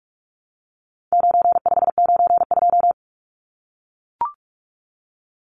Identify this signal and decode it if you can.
Morse code, heard as a keyed tone, about 44 wpm, 700 Hz. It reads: 9592
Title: Numbers - 4-Digit 45wpm